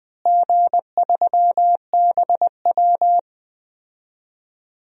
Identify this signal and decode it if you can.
Morse code, heard as a keyed tone, about 19 wpm, 700 Hz. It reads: G3BW
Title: DXCC-100 Call Signs - 4 Characters - Repeated 3x - 20wpm